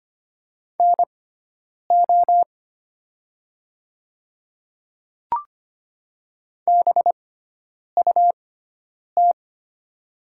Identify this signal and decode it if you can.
Morse code, heard as a keyed tone, about 25 wpm, spaced as for 9 wpm, 700 Hz. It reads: NO BUT
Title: 6x Character Spacing - Top 100 Words - 25wpm